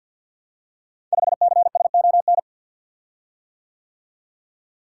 Morse code, heard as a keyed tone, about 47 wpm, 700 Hz. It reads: 5CSON
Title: Code Groups - 5 Letter:Numbers - 50wpm